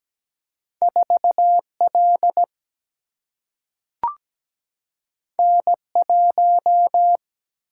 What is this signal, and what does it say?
Morse code, heard as a keyed tone, about 17 wpm, 700 Hz. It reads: 4L N1
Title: Single-Letter Single-Number 17wpm